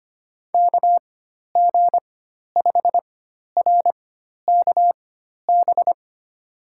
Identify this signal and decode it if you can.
Morse code, heard as a keyed tone, about 25 wpm, spaced as for 12 wpm, 700 Hz. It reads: KG5RKB
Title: USA Call Signs - 6 Characters - 4x Character Spacing - 25wpm